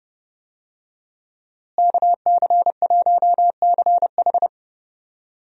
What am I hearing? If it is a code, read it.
Morse code, heard as a keyed tone, about 30 wpm, 700 Hz. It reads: KC1CH
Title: USA Call Signs - 5 Characters - Repeated 3x - 30wpm